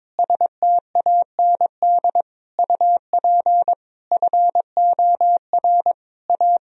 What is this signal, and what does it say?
Morse code, heard as a keyed tone, about 22 wpm, 700 Hz. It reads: STAND UP FOR A
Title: Sentences from Top 1000 Words - Review Part 1 - 22wpm